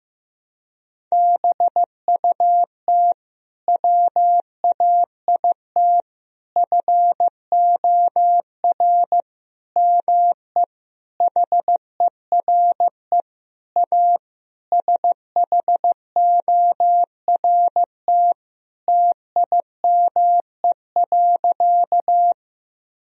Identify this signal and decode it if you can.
Morse code, heard as a keyed tone, about 15 wpm, 700 Hz. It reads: BUT WAIT FOR ME HERE A SHORT TIME.
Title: Sentences from Top 1000 Words - Review Part 2 - 15wpm